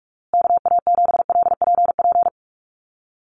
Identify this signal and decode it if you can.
Morse code, heard as a keyed tone, about 44 wpm, 700 Hz. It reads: KA7LPP
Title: USA Call Signs Repeated 2x 45wpm